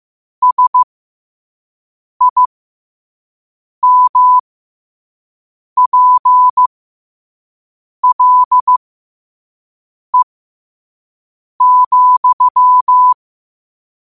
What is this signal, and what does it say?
Morse code, heard as a keyed tone, about 15 wpm, spaced as for 5 wpm, 1000 Hz. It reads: SIMPLE,
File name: qotdfiles / 5wpm